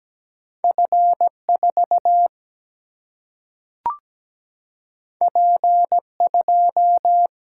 Morse code, heard as a keyed tone, about 17 wpm, 700 Hz. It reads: F4 P2
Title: Single-Letter Single-Number 17wpm